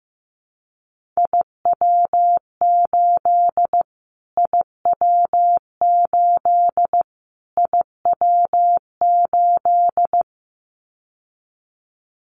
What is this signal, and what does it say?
Morse code, heard as a keyed tone, about 15 wpm, 700 Hz. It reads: IW8 IW8 IW8
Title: Callsign Prefixes for Top 25 DXCC - Repeated 3x - 15wpm